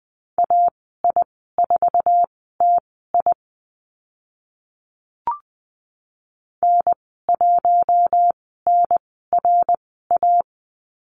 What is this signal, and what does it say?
Morse code, heard as a keyed tone, about 20 wpm, spaced as for 14 wpm, 700 Hz. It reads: AI4TI N1NRA
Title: 055 Lesson 10g - Letter r - Call Signs 20wpm